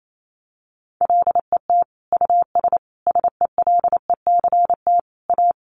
Morse code, heard as a keyed tone, about 28 wpm, 700 Hz. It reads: LET US SELECT A
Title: Sentences from Top 1300 Words 28wpm